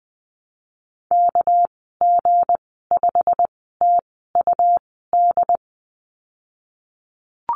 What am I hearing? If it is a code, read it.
Morse code, heard as a keyed tone, about 20 wpm, spaced as for 15 wpm, 700 Hz. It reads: KG5TUD
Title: 207 Lesson 33g - Letter j - Call Signs 20wpm